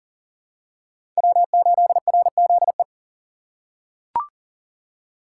Morse code, heard as a keyed tone, about 37 wpm, 690 Hz. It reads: W8RZE
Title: USA Call Signs - 2x Character Spacing - 40wpm